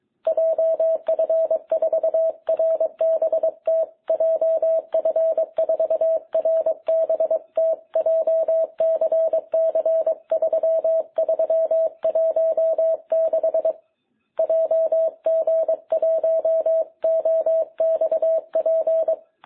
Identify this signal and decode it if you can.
Morse code, heard as a keyed tone, about 22 wpm, 660 Hz. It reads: JF4RBT JF4RBT JCC3316 JG1OXP
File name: jh8cbh / 2021cw